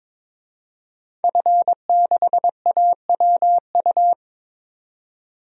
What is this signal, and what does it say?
Morse code, heard as a keyed tone, about 22 wpm, 700 Hz. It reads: F6AWU
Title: DXCC-100 Call Signs - 5 Characters - 22wpm